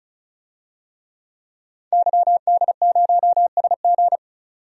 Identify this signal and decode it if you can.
Morse code, heard as a keyed tone, about 36 wpm, 700 Hz. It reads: YD0SG
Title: DXCC-100 Call Signs - 5 Characters - 35wpm